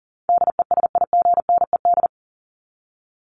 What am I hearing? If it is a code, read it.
Morse code, heard as a keyed tone, about 40 wpm, 700 Hz. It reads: DESIGNED
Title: ED Suffix - Top 100 Words with Base Words 40wpm